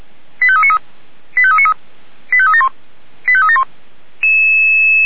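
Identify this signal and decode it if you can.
Five-tone selective call (ZVEI), heard as quick runs of five tones, tones of about 70 ms. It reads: 85393 85393 85371 85371
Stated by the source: ZVEI five-tone call 89131 received by SDR.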